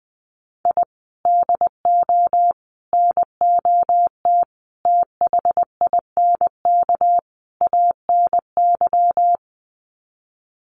Morse code, heard as a keyed tone, about 20 wpm, 700 Hz. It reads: I DO NOT THINK ANY
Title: Sets of 5 Words Encore 20wpm